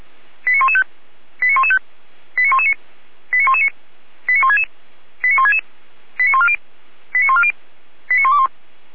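Five-tone selective call (ZVEI), heard as quick runs of five tones, tones of about 70 ms. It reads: 89115 89115 89119 89119 89166 89166 89140 89140 89121
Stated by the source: ZVEI five-tone selective call received on an SDR.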